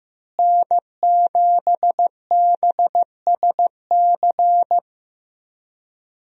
Morse code, heard as a keyed tone, about 16 wpm, 700 Hz. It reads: N7BSC
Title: USA Call Signs - 5 Characters - Repeated 2x - 15wpm